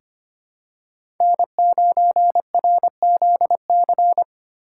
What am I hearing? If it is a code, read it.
Morse code, heard as a keyed tone, about 25 wpm, 700 Hz. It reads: N9RZC
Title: USA Call Signs - 5 Characters - 4x Character Spacing - 25wpm